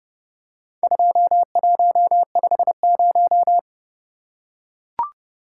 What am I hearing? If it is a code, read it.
Morse code, heard as a keyed tone, about 30 wpm, 700 Hz. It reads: 2150